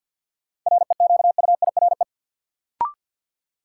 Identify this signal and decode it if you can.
Morse code, heard as a keyed tone, about 47 wpm, 700 Hz. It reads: REQUIRE
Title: Sentences from Top 900 Words - New Word Drill - 50wpm